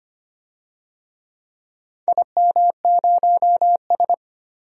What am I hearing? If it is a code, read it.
Morse code, heard as a keyed tone, about 25 wpm, 700 Hz. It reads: IM0S
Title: DXCC-100 Call Signs - 4 Characters - 25wpm